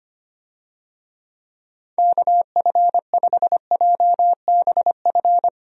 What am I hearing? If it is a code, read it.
Morse code, heard as a keyed tone, about 25 wpm, 700 Hz. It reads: KF5JBF
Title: USA Call Signs - 6 Characters - Repeated 2x - 25wpm